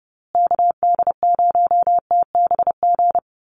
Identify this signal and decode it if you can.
Morse code, heard as a keyed tone, about 29 wpm, 700 Hz. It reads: KD0TBG
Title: USA Call Signs Repeated 3x 30wpm